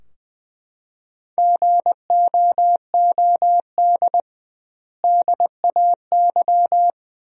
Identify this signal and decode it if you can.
Morse code, heard as a keyed tone, about 21 wpm, 700 Hz. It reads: GOOD DAY
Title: 2x Word-Spacing - Sentences from Top 200 Words - 20wpm